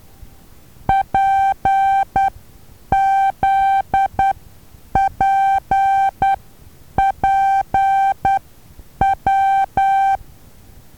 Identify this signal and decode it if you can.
Morse code, heard as a keyed tone, about 10 wpm, spaced as for 8 wpm, 780 Hz. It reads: PZPPW
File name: Taviro HA6VA B